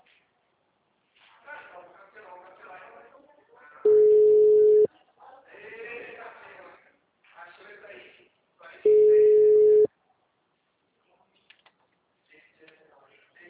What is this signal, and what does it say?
Ringback tone (Europe, 425 Hz), with speech.